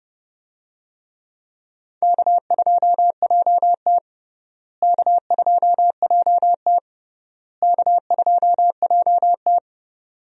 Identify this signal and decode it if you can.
Morse code, heard as a keyed tone, about 32 wpm, 700 Hz. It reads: K2JT K2JT K2JT
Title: USA Call Signs - 4 Characters - Repeated 3x - 30wpm